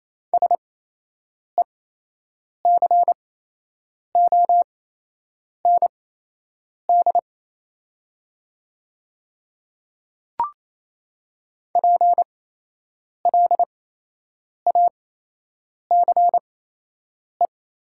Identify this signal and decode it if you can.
Morse code, heard as a keyed tone, about 28 wpm, spaced as for 8 wpm, 700 Hz. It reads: SECOND PLACE
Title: 8x Character Spacing - Top 300 Words - 28wpm